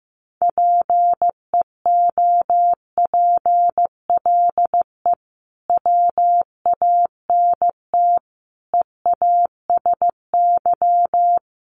Morse code, heard as a keyed tone, about 15 wpm, 700 Hz. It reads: PEOPLE WANT EASY
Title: Sentences from Top 1500 Words 15wpm